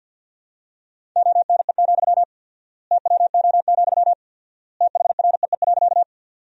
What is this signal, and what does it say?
Morse code, heard as a keyed tone, about 47 wpm, 700 Hz. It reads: ONE, TWO, THREE.
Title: 4x Word-Spacing - Sentences from Top 200 Words - 50wpm